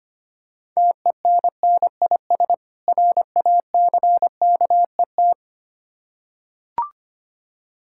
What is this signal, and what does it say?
Morse code, heard as a keyed tone, about 25 wpm, 700 Hz. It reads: TENNIS RACKET
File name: practice / 25wpm / Difficult Words with Hints - 25wpm